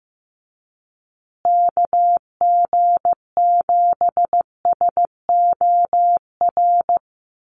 Morse code, heard as a keyed tone, about 15 wpm, 700 Hz. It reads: KG7SOR
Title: USA Call Signs Repeated 2x 15wpm